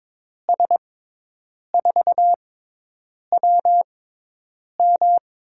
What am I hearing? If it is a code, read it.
Morse code, heard as a keyed tone, about 22 wpm, 700 Hz. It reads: S 4 W M
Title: USA Call Signs - 6x Character Spacing - 22wpm